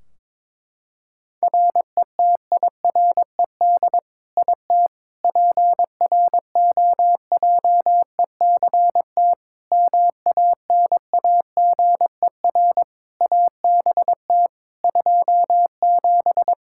Morse code, heard as a keyed tone, about 22 wpm, 700 Hz. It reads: RETIRED IT PROJECT MANAGER ABT 27
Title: Practice QSO - Exchange 3 - More Info - 22wpm